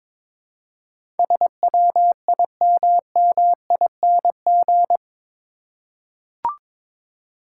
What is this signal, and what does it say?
Morse code, heard as a keyed tone, about 21 wpm, 700 Hz. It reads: SWIMMING
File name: practice / 22wpm / Eight Letter Words - 22wpm